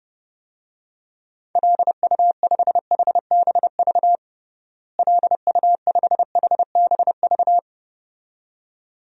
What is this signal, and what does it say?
Morse code, heard as a keyed tone, about 30 wpm, 700 Hz. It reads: LU5HBV LU5HBV
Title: DXCC-100 Call Signs - 6 Characters - Repeated 2x - 30wpm